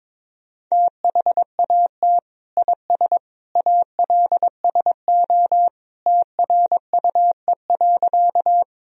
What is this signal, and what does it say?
Morse code, heard as a keyed tone, about 21 wpm, 700 Hz. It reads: THAT IS ALSO TRUE.
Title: Sentences from Top 1000 Words - Review Part 1 - 22wpm